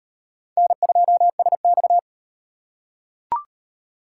Morse code, heard as a keyed tone, about 36 wpm, 700 Hz. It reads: N2SX